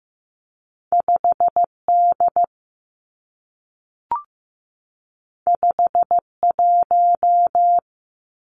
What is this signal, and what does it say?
Morse code, heard as a keyed tone, about 15 wpm, 700 Hz. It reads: 5 D 5 1
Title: Two Challenging Letter-Number Combinations 15wpm